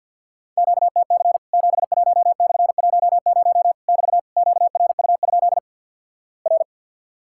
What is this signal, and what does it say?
Morse code, heard as a keyed tone, about 50 wpm, 700 Hz. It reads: QTC 71/10 <BT> QRU? R
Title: Worked All Europe DX Contest - QTC Exchange with Repeats - 50wpm